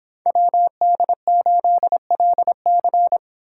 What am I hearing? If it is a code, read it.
Morse code, heard as a keyed tone, about 25 wpm, 700 Hz. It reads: WD8LC